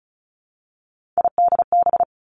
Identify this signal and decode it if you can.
Morse code, heard as a keyed tone, about 36 wpm, 700 Hz. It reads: IDB
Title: Three Challenging Letter-Number Combinations 35wpm